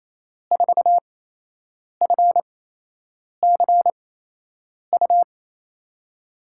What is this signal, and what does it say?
Morse code, heard as a keyed tone, about 28 wpm, spaced as for 8 wpm, 700 Hz. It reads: 4FCU
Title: USA Call Signs - 8x Character Spacing - 28wpm